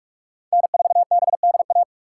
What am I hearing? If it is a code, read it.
Morse code, heard as a keyed tone, about 59 wpm, 700 Hz. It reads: N4BDA